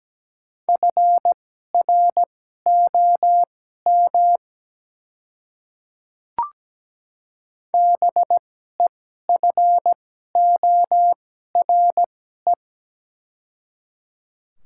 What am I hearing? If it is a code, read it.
Morse code, heard as a keyed tone, about 17 wpm, spaced as for 12 wpm, 700 Hz. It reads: FROM BEFORE
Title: 2x Character Spacing - Top 200 Words - 17wpm